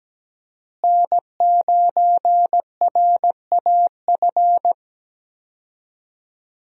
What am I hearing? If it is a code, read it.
Morse code, heard as a keyed tone, about 17 wpm, 700 Hz. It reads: N9RAF
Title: USA Call Signs - 5 Characters - Repeated 2x - 17wpm